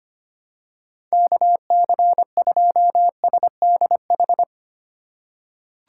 Morse code, heard as a keyed tone, about 25 wpm, 700 Hz. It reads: KC2SDH
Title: USA Call Signs - 6 Characters - 3x Character Spacing - 25wpm